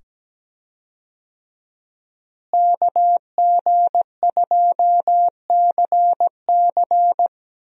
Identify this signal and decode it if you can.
Morse code, heard as a keyed tone, about 17 wpm, 700 Hz. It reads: KG2CC
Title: USA Call Signs - 5 Characters - Repeated 2x - 17wpm